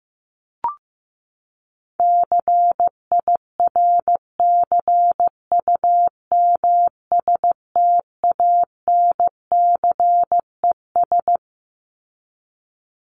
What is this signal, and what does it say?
Morse code, heard as a keyed tone, about 15 wpm, 700 Hz. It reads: CIRCUMSTANCES
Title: ES:S Suffix - Top 100 Words with Base Words 15wpm